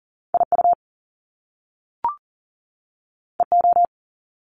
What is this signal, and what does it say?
Morse code, heard as a keyed tone, about 40 wpm, 700 Hz. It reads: IU EO
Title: Top Two Letter Combinations in Words 40wpm